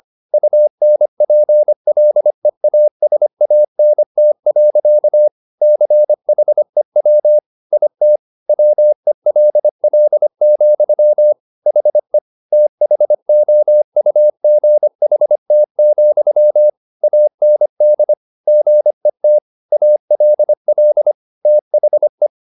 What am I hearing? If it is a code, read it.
Morse code, heard as a keyed tone, about 25 wpm, 600 Hz. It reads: UNPLEASANT. CHEW IT WELL, HE THOUGHT, AND GET ALL THE
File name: the old man and the sea part 1 25wpm